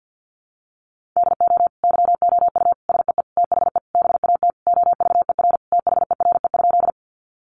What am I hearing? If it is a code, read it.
Morse code, heard as a keyed tone, about 50 wpm, 700 Hz. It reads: DO YOU SEE THE BAT OVER THERE?
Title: Sentences from Top 1500 Words 50wpm